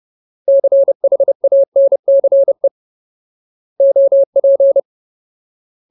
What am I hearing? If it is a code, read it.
Morse code, heard as a keyed tone, about 30 wpm, 550 Hz. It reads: CHANCE OP
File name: Ultimate Word List / 30WPM Ultimate